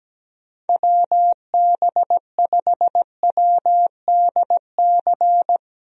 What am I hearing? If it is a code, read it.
Morse code, heard as a keyed tone, about 17 wpm, 700 Hz. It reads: WB5WDC
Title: USA Call Signs - 6 Characters - 17wpm